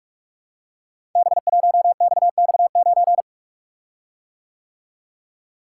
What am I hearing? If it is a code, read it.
Morse code, heard as a keyed tone, about 44 wpm, 700 Hz. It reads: B1XX9
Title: Code Groups - 5 Letter:Numbers - 45wpm